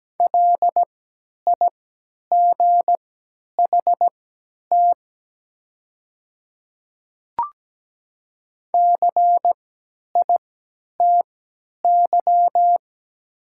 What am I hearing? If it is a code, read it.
Morse code, heard as a keyed tone, about 17 wpm, spaced as for 10 wpm, 700 Hz. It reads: LIGHT CITY
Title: 3x Character Spacing - Top 300 Words - 17wpm